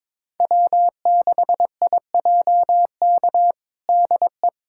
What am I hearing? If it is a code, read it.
Morse code, heard as a keyed tone, about 22 wpm, 700 Hz. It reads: W6IJK DE
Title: USA Callsign DE Callsign - 5 Characters - 22wpm